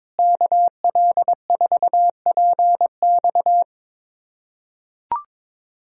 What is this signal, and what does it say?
Morse code, heard as a keyed tone, about 22 wpm, 700 Hz. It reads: KL4PX
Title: USA Callsign DE Callsign - 5 Characters - 22wpm